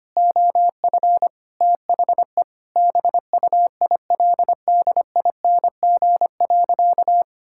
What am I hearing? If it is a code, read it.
Morse code, heard as a keyed tone, about 25 wpm, 700 Hz. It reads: OF THE BUILDING.
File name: practice / 25wpm / Sentences with 3-Word Phrases - 25wpm